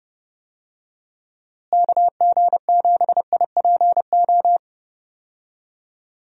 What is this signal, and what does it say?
Morse code, heard as a keyed tone, about 29 wpm, 700 Hz. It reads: KG7IPO
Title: USA Call Signs - 6 Characters - Repeated 2x - 30wpm